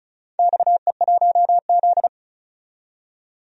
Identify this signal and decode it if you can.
Morse code, heard as a keyed tone, about 36 wpm, 700 Hz. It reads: XE1Z
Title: DXCC-100 Call Signs - 4 Characters - 35wpm